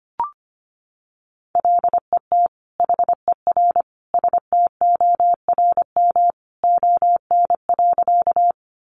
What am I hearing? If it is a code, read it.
Morse code, heard as a keyed tone, about 25 wpm, 700 Hz. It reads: LET HER STORM ON.
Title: Sentences from Top 1500 Words 25wpm